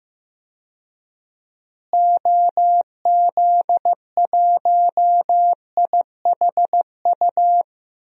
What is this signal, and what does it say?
Morse code, heard as a keyed tone, about 15 wpm, 700 Hz. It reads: OZ1IHU
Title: DXCC-100 Call Signs - 6 Characters - Repeated 2x - 15wpm